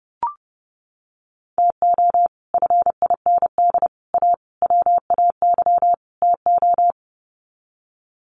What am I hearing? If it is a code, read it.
Morse code, heard as a keyed tone, about 30 wpm, 700 Hz. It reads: TO FIND A WAY TO
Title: Sets of 5 Words 30wpm